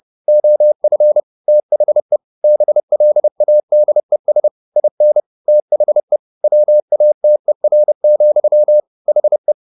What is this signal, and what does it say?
Morse code, heard as a keyed tone, about 30 wpm, 600 Hz. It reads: OF THE BLADES IN THE WATER, HE
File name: the old man and the sea / the old man and the sea 1 30wpm